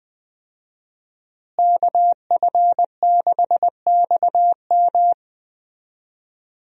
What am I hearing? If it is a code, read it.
Morse code, heard as a keyed tone, about 20 wpm, 700 Hz. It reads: KF6XM
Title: USA Call Signs - 5 Characters - Repeated 3x - 20wpm